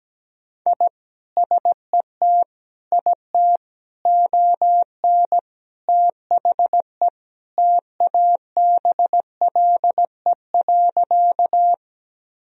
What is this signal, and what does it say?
Morse code, heard as a keyed tone, about 17 wpm, 700 Hz. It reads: I SET IT ON THE TABLE.
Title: Sentences from Top 1000 Words - Review Part 2 - 17wpm